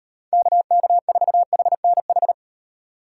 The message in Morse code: KK4HNH

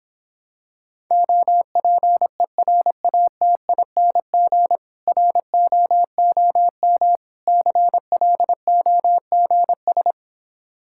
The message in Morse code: OPERATING ROOM CLOGS

